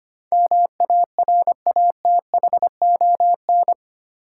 Morse code, heard as a keyed tone, about 25 wpm, 690 Hz. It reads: MARATHON
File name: 8 Letter Words - First 3 Letters Given - 25wpm